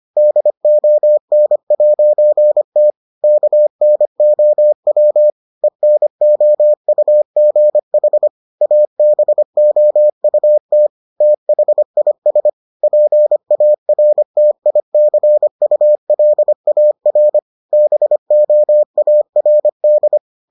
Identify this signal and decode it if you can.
Morse code, heard as a keyed tone, about 25 wpm, 600 Hz. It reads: DON'T KNOW ENOUGH ABOUT THIS PARTICULAR BOARD